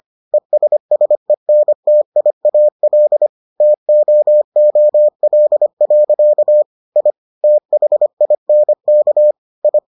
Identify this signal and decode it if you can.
Morse code, heard as a keyed tone, about 25 wpm, 600 Hz. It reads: ESSENTIAL TOOL. I THINK I